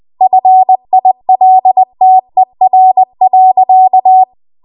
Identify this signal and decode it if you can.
Morse code, heard as a keyed tone, about 20 wpm, 750 Hz. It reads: FILTER.
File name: Morse / 20WPM